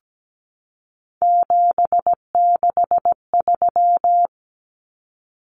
Morse code, heard as a keyed tone, about 17 wpm, 700 Hz. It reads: 763
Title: Numbers - 3-Digit 17wpm